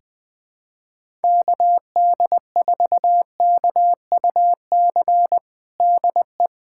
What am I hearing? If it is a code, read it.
Morse code, heard as a keyed tone, about 20 wpm, 700 Hz. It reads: KD4KUC DE
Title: USA Callsign DE Callsign 20wpm